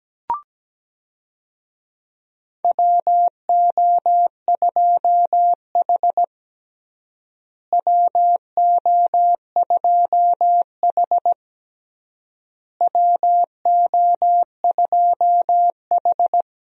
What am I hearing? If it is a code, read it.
Morse code, heard as a keyed tone, about 17 wpm, 700 Hz. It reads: WO2H WO2H WO2H
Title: USA Call Signs - 4 Characters - Repeated 3x - 17wpm